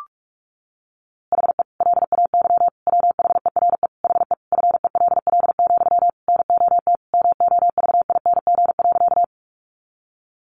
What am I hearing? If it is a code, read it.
Morse code, heard as a keyed tone, about 44 wpm, 700 Hz. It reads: HE LAY WHERE HE FELL, NOT MOVING.
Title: A Sound Of Thunder 45wpm